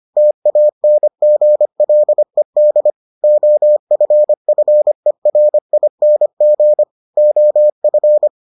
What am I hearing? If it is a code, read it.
Morse code, heard as a keyed tone, about 25 wpm, 600 Hz. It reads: TANGLED OFFERING OF